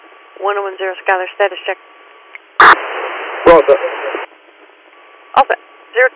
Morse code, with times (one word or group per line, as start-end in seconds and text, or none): none